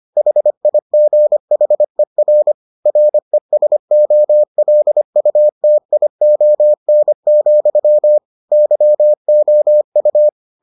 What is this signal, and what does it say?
Morse code, heard as a keyed tone, about 25 wpm, 600 Hz. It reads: HIGHER RESOLUTION, YOU